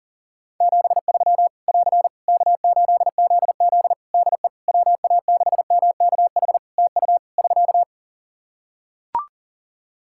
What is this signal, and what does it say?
Morse code, heard as a keyed tone, about 37 wpm, 700 Hz. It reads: 73 <AR> K8ZZ DE WA6MKH TU <SK>
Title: K1USN SST QSO Practice 40wpm